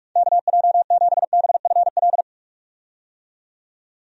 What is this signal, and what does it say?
Morse code, heard as a keyed tone, about 46 wpm, 700 Hz. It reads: KJ7BFL
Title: USA Call Signs - 6 Characters - 45wpm